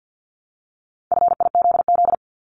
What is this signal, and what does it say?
Morse code, heard as a keyed tone, about 50 wpm, 700 Hz. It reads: FIZZ
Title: Multi-letter Phonemes Repeated 3x 50wpm